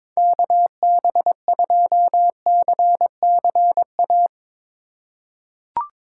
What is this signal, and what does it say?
Morse code, heard as a keyed tone, about 22 wpm, 700 Hz. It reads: KB2CCA
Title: USA Call Signs - 6 Characters - 8x Character Spacing - 22wpm